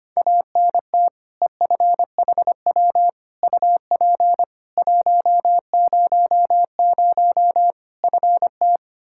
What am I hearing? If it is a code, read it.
Morse code, heard as a keyed tone, about 25 wpm, 700 Hz. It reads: ANT EFHW UP 100 FT
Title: Practice QSO - Exchange 2 - Info - 25wpm